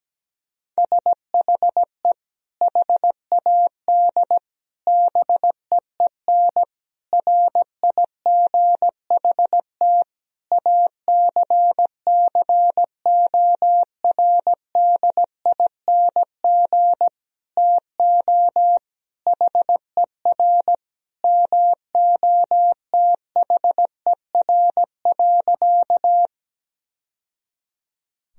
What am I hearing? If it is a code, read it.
Morse code, heard as a keyed tone, about 17 wpm, 700 Hz. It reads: SHE HAD BEEN RIGHT ACCORDING TO HER MOTHER.
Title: Sentences from Top 1000 Words - Review Part 4 - 17wpm